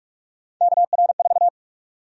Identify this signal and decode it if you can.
Morse code, heard as a keyed tone, about 44 wpm, 700 Hz. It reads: KR4
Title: USA Call Sign Prefixes - 45wpm